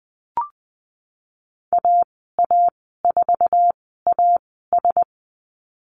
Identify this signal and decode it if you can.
Morse code, heard as a keyed tone, about 20 wpm, spaced as for 15 wpm, 700 Hz. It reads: AA4AS
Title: 069 Lesson 12g - Letter d - Call Signs 20wpm